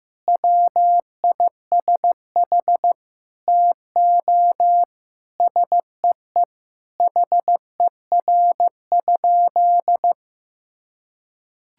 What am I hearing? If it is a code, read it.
Morse code, heard as a keyed tone, about 15 wpm, 700 Hz. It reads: WISH TO SEE HER?
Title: Sentences from Top 1600 Words 15wpm